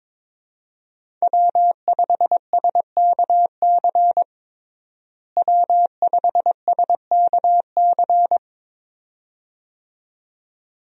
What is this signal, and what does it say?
Morse code, heard as a keyed tone, about 22 wpm, 700 Hz. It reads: W5SKC W5SKC
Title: USA Call Signs - 5 Characters - Repeated 2x - 22wpm